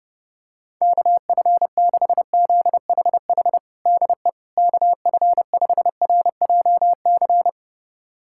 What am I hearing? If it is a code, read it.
Morse code, heard as a keyed tone, about 31 wpm, 700 Hz. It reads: KF6ZHH DE KF5RJC
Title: USA Callsign DE Callsign 30wpm